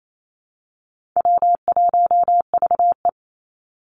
Morse code, heard as a keyed tone, about 28 wpm, 700 Hz. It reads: W1VE
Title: Easy Call Signs Repeated 2x - 28wpm